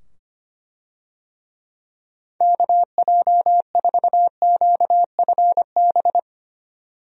Morse code, heard as a keyed tone, about 25 wpm, 700 Hz. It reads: KJ4QFB
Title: USA Call Signs - 6 Characters - 25wpm